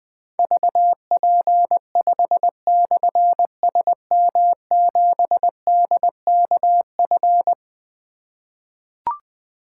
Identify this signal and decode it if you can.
Morse code, heard as a keyed tone, about 20 wpm, 700 Hz. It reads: VP5/SM7DKF
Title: Difficult Call Signs 20wpm